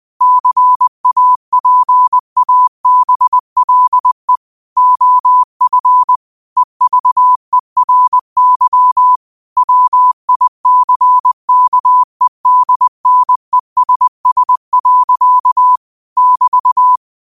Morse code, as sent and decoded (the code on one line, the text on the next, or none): -.-. .- .--. .- -... .-.. / --- ..-. / . ...- . .-. -.-- / .-- .. -.-. -.- . -.. -. ... ... .-.-.- / -...-
CAPABLE OF EVERY WICKEDNESS. <BT>